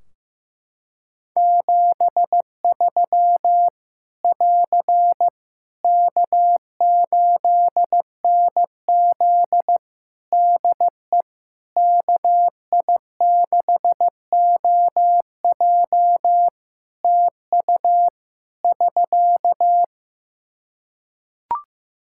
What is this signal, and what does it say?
Morse code, heard as a keyed tone, about 15 wpm, 700 Hz. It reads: 73 <AR> K8NZ DE KI6OJ TU <SK>
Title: Practice QSO - Exchange 4 - Ending - 15wpm